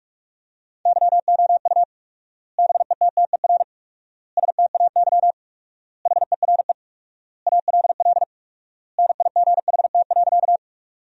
Morse code, heard as a keyed tone, about 47 wpm, 700 Hz. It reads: YOU BETTER STAY HERE ALL NIGHT.
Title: 4x Word-Spacing - Sentences from Top 500 Words - 45wpm